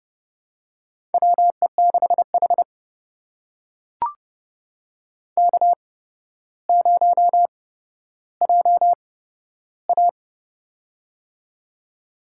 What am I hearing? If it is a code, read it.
Morse code, heard as a keyed tone, about 30 wpm, spaced as for 13 wpm, 700 Hz. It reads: WE6H K0JA
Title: USA Call Signs - 4 Characters - 8x Character Spacing - 30wpm